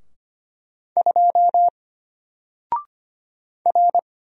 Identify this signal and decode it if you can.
Morse code, heard as a keyed tone, about 25 wpm, 700 Hz. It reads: 2 R
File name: Single-Letter-Number-25wpm